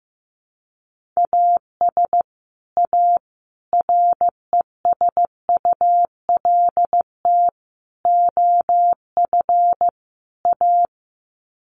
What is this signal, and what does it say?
Morse code, heard as a keyed tone, about 15 wpm, 700 Hz. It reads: AS A RESULT OF A
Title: Sets of 5 Words Encore 15wpm